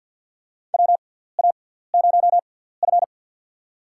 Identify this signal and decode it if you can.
Morse code, heard as a keyed tone, about 52 wpm, spaced as for 17 wpm, 700 Hz. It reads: WA0F
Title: USA Call Signs - 6x Character Spacing - 50wpm